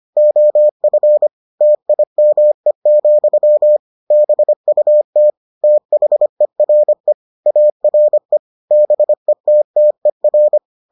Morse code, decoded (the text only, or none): OF TIME, BUT THERE ARE BETTER